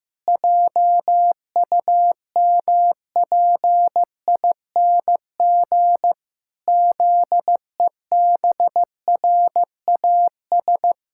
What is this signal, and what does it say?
Morse code, heard as a keyed tone, about 15 wpm, 700 Hz. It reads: JUMPING ZEBRAS
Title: Pangram Challenge v2 - 15wpm